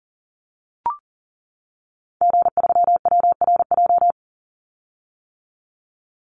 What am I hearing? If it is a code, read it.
Morse code, heard as a keyed tone, about 40 wpm, 700 Hz. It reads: G3WRJ
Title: Call Signs - 40wpm